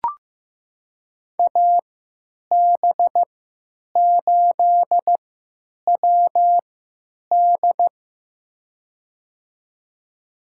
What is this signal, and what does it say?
Morse code, heard as a keyed tone, about 15 wpm, spaced as for 9 wpm, 700 Hz. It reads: AB8WD
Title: USA Call Signs - 5 Characters - 3x Character Spacing - 15wpm